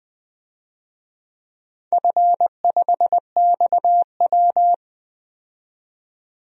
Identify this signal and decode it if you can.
Morse code, heard as a keyed tone, about 20 wpm, 700 Hz. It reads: F5XW
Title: DXCC-100 Call Signs - 4 Characters - 20wpm